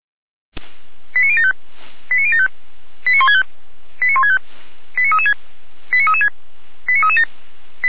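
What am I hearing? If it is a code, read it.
Five-tone selective call (ZVEI), heard as quick runs of five tones, tones of about 70 ms. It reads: 89075 89075 89165 89165 89336 89336 89337
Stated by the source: ZVEI five-tone selective call received on SDR, Feuerwehr Orlamünde.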